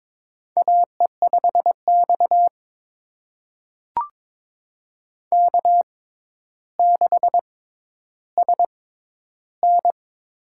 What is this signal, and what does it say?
Morse code, heard as a keyed tone, about 22 wpm, spaced as for 11 wpm, 700 Hz. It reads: AE5X K6SN
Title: USA Call Signs - 4 Characters - 6x Character Spacing - 22wpm